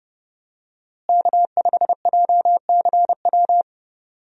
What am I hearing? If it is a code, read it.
Morse code, heard as a keyed tone, about 31 wpm, 700 Hz. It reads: K5JCW
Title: USA Call Signs - 5 Characters - Repeated 3x - 30wpm